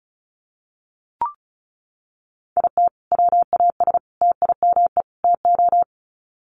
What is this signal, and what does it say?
Morse code, heard as a keyed tone, about 34 wpm, 700 Hz. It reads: IT WAS TIME TO
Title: Sets of 4 Words 35wpm